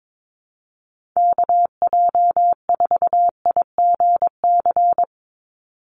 Morse code, heard as a keyed tone, about 22 wpm, 700 Hz. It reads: KJ4IGC
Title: USA Call Signs 22wpm